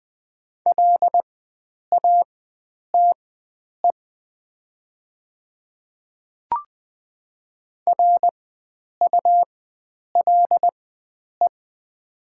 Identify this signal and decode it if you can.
Morse code, heard as a keyed tone, about 20 wpm, spaced as for 9 wpm, 690 Hz. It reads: LATE RULE
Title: 4x Character Spacing - Top 500 Words - 20wpm